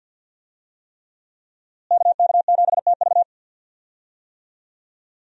Morse code, heard as a keyed tone, about 52 wpm, 700 Hz. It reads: KK7TV
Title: USA Call Signs - 5 Characters - Repeated 3x - 50wpm